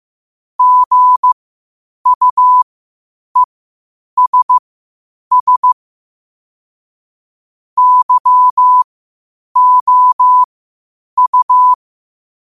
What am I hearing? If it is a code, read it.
Morse code, heard as a keyed tone, about 15 wpm, spaced as for 8 wpm, 1000 Hz. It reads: GUESS YOU
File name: qotdfiles / 8wpm